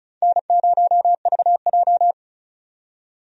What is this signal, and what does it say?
Morse code, heard as a keyed tone, about 36 wpm, 700 Hz. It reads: N0VJ
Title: USA Call Signs - 4 Characters - 35wpm